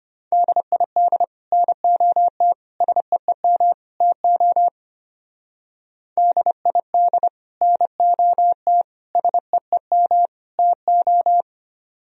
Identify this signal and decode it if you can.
Morse code, heard as a keyed tone, about 27 wpm, 700 Hz. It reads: DID NOT SEEM TO DID NOT SEEM TO
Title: Speed-Racing - Sets of 4 Words 25wpm